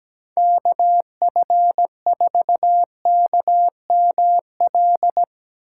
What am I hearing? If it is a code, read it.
Morse code, heard as a keyed tone, about 18 wpm, 700 Hz. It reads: KF4KML
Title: Speed Racing - USA 2x3 Call Signs 17wpm